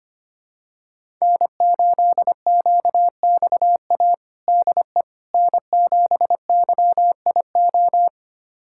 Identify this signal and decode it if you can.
Morse code, heard as a keyed tone, about 25 wpm, 700 Hz. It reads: N8QXA DE N7YIO
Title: USA Callsign DE Callsign - 5 Characters - 25wpm